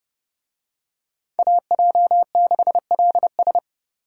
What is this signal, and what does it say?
Morse code, heard as a keyed tone, about 30 wpm, 700 Hz. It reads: AJ6LS